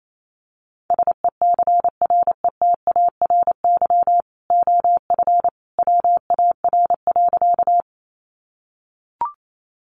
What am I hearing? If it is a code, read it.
Morse code, heard as a keyed tone, about 28 wpm, 700 Hz. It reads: SECRETARY OF WAR.
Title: Sentences from Top 1300 Words 28wpm